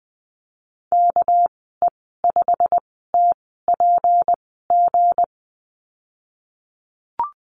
Morse code, heard as a keyed tone, about 20 wpm, spaced as for 15 wpm, 700 Hz. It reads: KE5TPG
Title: 207 Lesson 33g - Letter j - Call Signs 20wpm